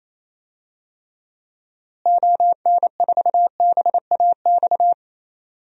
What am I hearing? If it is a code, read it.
Morse code, heard as a keyed tone, about 28 wpm, 700 Hz. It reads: ON4BAX
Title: DXCC-100 Call Signs - 6 Characters - Repeated 2x - 28wpm